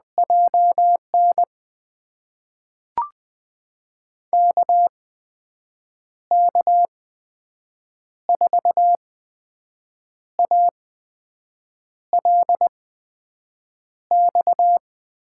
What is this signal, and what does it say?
Morse code, heard as a keyed tone, about 20 wpm, spaced as for 6 wpm, 700 Hz. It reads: JN KK4ALX
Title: USA Call Signs - 6 Characters - 8x Character Spacing - 20wpm